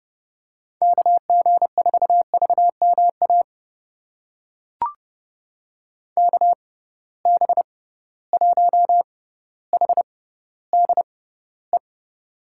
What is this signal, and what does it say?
Morse code, heard as a keyed tone, about 30 wpm, spaced as for 15 wpm, 700 Hz. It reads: KG4VMA KB1HDE
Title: USA Call Signs - 6 Characters - 6x Character Spacing - 30wpm